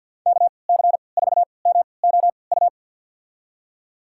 Morse code, heard as a keyed tone, about 48 wpm, spaced as for 28 wpm, 700 Hz. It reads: KC4MOU
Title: USA Call Signs - 6 Characters - 3x Character Spacing - 50wpm